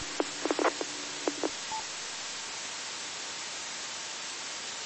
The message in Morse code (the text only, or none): none